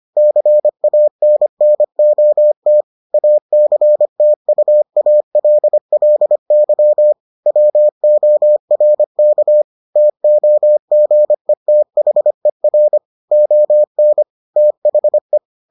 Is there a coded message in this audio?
Yes: Morse code from 0.2 to 15.4 s, heard as a keyed tone, about 25 wpm, 600 Hz. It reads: CANNOT ACTUALLY WORK TOGETHER ON THE